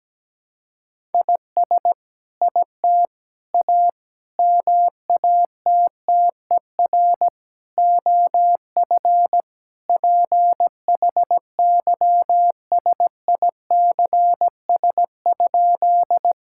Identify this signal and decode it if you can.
Morse code, heard as a keyed tone, about 17 wpm, 700 Hz. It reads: IS IT A MATTER OF PHYSICS?
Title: Sentences from Top 1000 Words - Review Part 5 - 17wpm